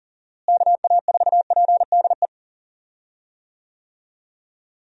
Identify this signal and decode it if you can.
Morse code, heard as a keyed tone, about 37 wpm, 700 Hz. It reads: KA4PDE
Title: USA Call Signs - 6 Characters - Repeated 2x - 40wpm